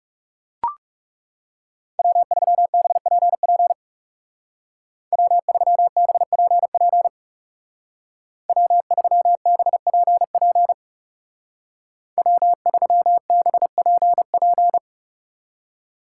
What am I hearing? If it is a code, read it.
Morse code, heard as a keyed tone, about 40 wpm, 700 Hz. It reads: W3BPP W3BPP W3BPP W3BPP